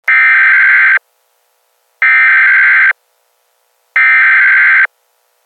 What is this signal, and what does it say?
Emergency Alert System (SAME) tones (header bursts), heard as harsh buzzing bursts. It is heard from 0.1 to 4.9 s.